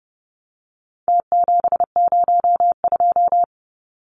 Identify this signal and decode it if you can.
Morse code, heard as a keyed tone, about 30 wpm, 700 Hz. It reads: T702
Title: Numbers - Contest Style - 4-Digit 30wpm